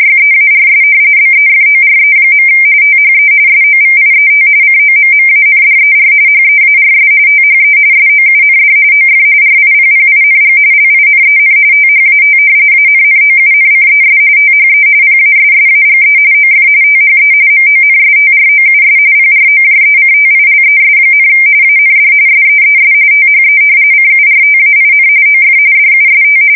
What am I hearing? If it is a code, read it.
RTTY, about 45.45 baud. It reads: YEARS BEFORE. THE SOVIET COMMANDER HEARS THE SOUND OF RIPPING LINEN THAT ANNOUNCES THE PASSAGE OF THE MONSTROUS PROJECTILES, AND SEES THE CURTAIN OF DUST
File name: fsk170 / battleshipnj